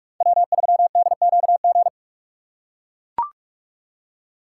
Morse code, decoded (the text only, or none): W3DQG